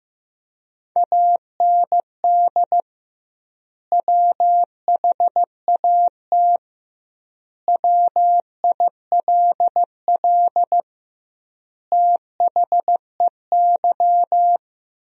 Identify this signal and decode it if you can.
Morse code, heard as a keyed tone, about 15 wpm, 700 Hz. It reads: AND WHAT WILL THEY
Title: 2x Word-Spacing - Sentences from Top 200 Words - 15wpm